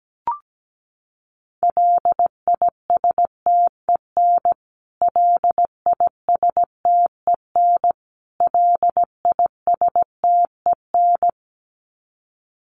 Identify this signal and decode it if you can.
Morse code, heard as a keyed tone, about 17 wpm, 700 Hz. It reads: LISTEN LISTEN LISTEN
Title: Top 500 Words Repeated 17wpm